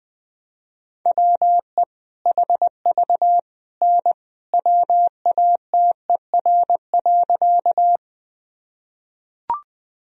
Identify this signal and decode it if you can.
Morse code, heard as a keyed tone, about 21 wpm, 700 Hz. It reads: WE HV N WATER.
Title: Sentences from Top 100 Words and Abbreviations 20wpm